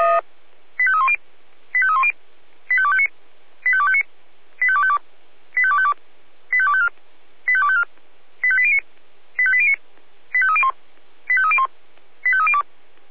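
Five-tone selective call (ZVEI), heard as quick runs of five tones, tones of about 70 ms. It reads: 85310 85310 85259 85259 85252 85252 85254 85254 85809 85809 85301 85301 85302
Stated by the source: Received via SDR, ZVEI five-tone call.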